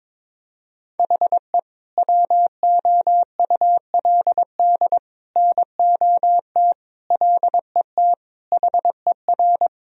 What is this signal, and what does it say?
Morse code, heard as a keyed tone, about 22 wpm, 700 Hz. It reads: HE WOULD NOT LET HER
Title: Sentences from Top 1000 Words - Review Part 2 - 22wpm